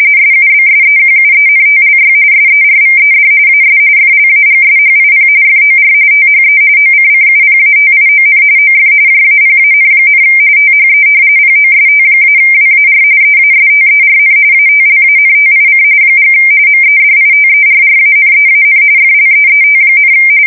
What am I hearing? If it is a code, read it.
RTTY, about 45.45 baud. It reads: SUBSEQUENTLY SERVED AS FLAGSHIP FOR THE OCCUPATION OF JAPAN, THE POST-WAR EASTERN ATLANTIC & MEDITERRANEAN FLEET,
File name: fsk170 / battleshipnj